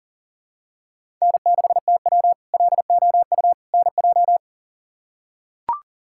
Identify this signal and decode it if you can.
Morse code, heard as a keyed tone, about 40 wpm, 700 Hz. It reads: N6TW LOU NJ
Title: K1USN SST QSO Practice - 40wpm